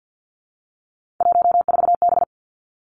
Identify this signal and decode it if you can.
Morse code, heard as a keyed tone, about 50 wpm, 700 Hz. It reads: 14B
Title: Three Challenging Letter-Number Combinations 50wpm